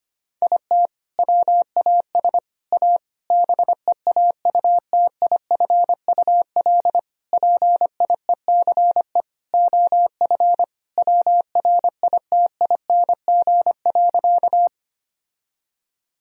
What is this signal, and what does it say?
Morse code, heard as a keyed tone, about 25 wpm, 700 Hz. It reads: IT WAS A BEAUTIFUL PIECE OF WRITING.
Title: Sentences from Top 1000 Words - Review Part 2 - 25wpm